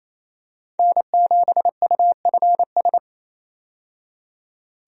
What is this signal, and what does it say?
Morse code, heard as a keyed tone, about 28 wpm, 700 Hz. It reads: N7UFS
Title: USA Call Signs - 5 Characters - 28wpm